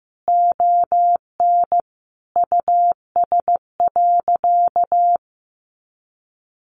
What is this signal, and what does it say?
Morse code, heard as a keyed tone, about 15 wpm, 700 Hz. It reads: ON US.
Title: Sentences from Top 1300 Words 15wpm